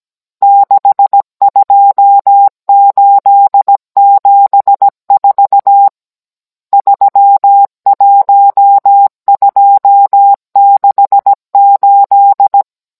Morse code, heard as a keyed tone, about 17 wpm, 800 Hz. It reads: 62874 31268